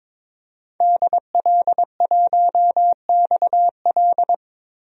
Morse code, heard as a keyed tone, about 22 wpm, 700 Hz. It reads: DL1XL
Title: DXCC-100 Call Signs - 5 Characters - 22wpm